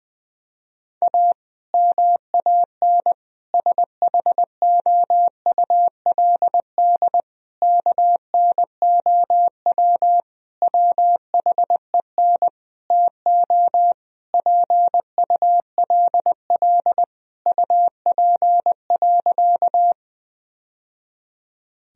Morse code, heard as a keyed tone, about 21 wpm, 700 Hz. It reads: A MAN SHOULD KNOW WHEN TO PULL UP.
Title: Sentences from Top 1000 Words - Review Part 2 - 20wpm